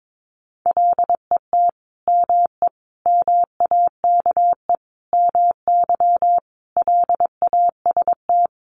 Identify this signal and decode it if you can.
Morse code, heard as a keyed tone, about 22 wpm, 700 Hz. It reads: LET ME MAKE MY LAST
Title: Sentences from Top 1300 Words 22wpm